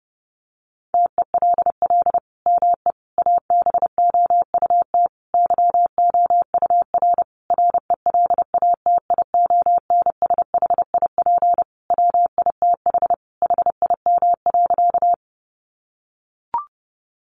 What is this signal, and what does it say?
Morse code, heard as a keyed tone, about 30 wpm, 700 Hz. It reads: TELL ME ABOUT YOUR RELATIONSHIP WITH HIM.
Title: Sentences from Top 1100 Words 30wpm